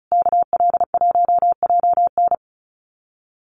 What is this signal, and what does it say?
Morse code, heard as a keyed tone, about 34 wpm, 700 Hz. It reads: KL1JN
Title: USA Call Signs Repeated 3x 35wpm